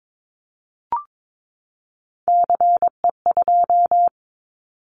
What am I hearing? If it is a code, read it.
Morse code, heard as a keyed tone, about 22 wpm, 700 Hz. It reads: CE2
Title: Callsign Prefixes for Top 100 DXCC - 22wpm